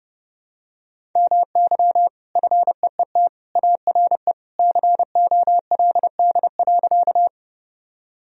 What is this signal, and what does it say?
Morse code, heard as a keyed tone, about 30 wpm, 700 Hz. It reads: MY FEET ARE COLD.
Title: Sentences from Top 1000 Words - Review Part 1 - 30wpm